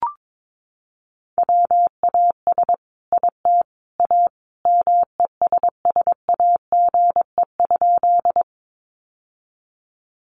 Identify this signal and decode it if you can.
Morse code, heard as a keyed tone, about 22 wpm, 700 Hz. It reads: WAS IT A MESSAGE?
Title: Sentences from Top 1100 Words 22wpm